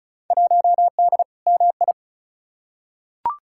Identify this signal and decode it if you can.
Morse code, heard as a keyed tone, about 34 wpm, 700 Hz. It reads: 1D MI
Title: Field Day Exchanges - 35wpm